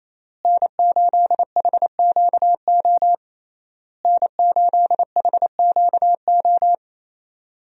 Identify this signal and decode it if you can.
Morse code, heard as a keyed tone, about 27 wpm, 700 Hz. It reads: N8HQO N8HQO
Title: USA Call Signs - 5 Characters - Repeated 3x - 28wpm